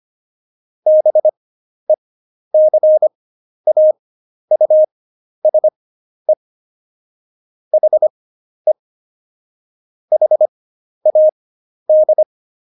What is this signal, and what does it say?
Morse code, heard as a keyed tone, about 25 wpm, spaced as for 11 wpm, 620 Hz. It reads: BECAUSE HE HAD
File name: Jokes-25-char-10-eff